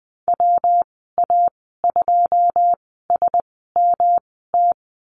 Morse code, heard as a keyed tone, about 20 wpm, spaced as for 15 wpm, 700 Hz. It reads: WA2SMT
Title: 147 Lesson 24g - Letter y - Call Signs 20wpm